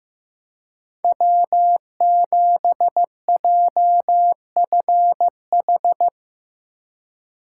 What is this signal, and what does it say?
Morse code, heard as a keyed tone, about 18 wpm, 700 Hz. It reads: W7JFH
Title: Speed Racing - USA 1x3 Call Signs 15wpm